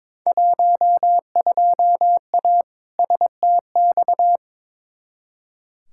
Morse code, heard as a keyed tone, about 21 wpm, 700 Hz. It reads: 12A STX
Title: Field Day Exchanges Encore - 22wpm